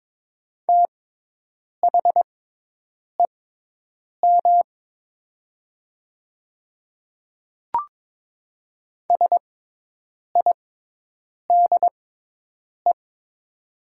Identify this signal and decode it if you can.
Morse code, heard as a keyed tone, about 22 wpm, spaced as for 8 wpm, 700 Hz. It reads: THEM SIDE
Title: 6x Character Spacing - Top 100 Words - 22wpm